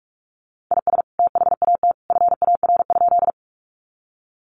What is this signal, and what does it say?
Morse code, heard as a keyed tone, about 45 wpm, 700 Hz. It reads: IS THAT FAR?
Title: Sentences from Top 300 Words 45wpm